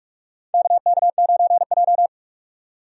Morse code, heard as a keyed tone, about 43 wpm, 700 Hz. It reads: KK9J
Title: USA Call Signs - 4 Characters - 45wpm